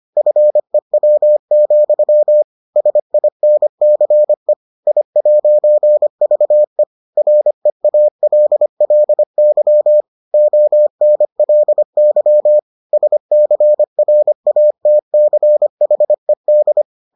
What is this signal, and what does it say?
Morse code, heard as a keyed tone, about 25 wpm, 600 Hz. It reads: FEW, SINCE I'VE REALLY ONLY SCRATCHED